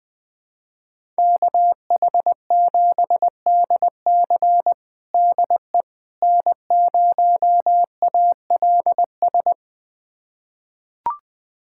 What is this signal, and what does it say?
Morse code, heard as a keyed tone, about 20 wpm, 700 Hz. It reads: KH7DC DE N0ALS
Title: USA Callsign DE Callsign - 5 Characters - 20wpm